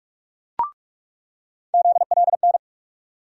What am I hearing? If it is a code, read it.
Morse code, heard as a keyed tone, about 46 wpm, 700 Hz. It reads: ZLN
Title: USA Call Sign Postfixes - 3 Letters 45wpm